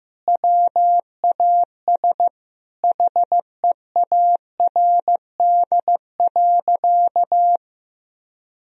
Morse code, heard as a keyed tone, about 15 wpm, 700 Hz. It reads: WAS HEARD.
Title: Sentences from Top 400 Words and Abbreviations 15wpm